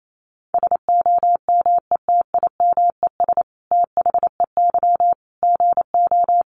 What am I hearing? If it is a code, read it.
Morse code, heard as a keyed tone, about 28 wpm, 700 Hz. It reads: SOMETIMES THEY GO
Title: Sentences from Top 600 Words 28wpm